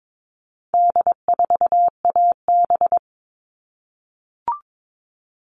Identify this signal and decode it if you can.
Morse code, heard as a keyed tone, about 22 wpm, 700 Hz. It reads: D4AB
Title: All Cut Numbers - 4-Digit 22wpm